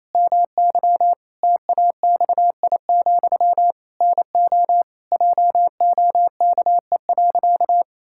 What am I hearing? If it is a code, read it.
Morse code, heard as a keyed tone, about 28 wpm, 700 Hz. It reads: MY TAXI, NO JOKE.
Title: Pangram Challenge v2 - 28wpm